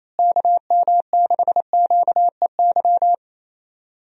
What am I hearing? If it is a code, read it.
Morse code, heard as a keyed tone, about 28 wpm, 700 Hz. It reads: KM6QEY
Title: USA Call Signs - 6 Characters - Repeated 2x - 28wpm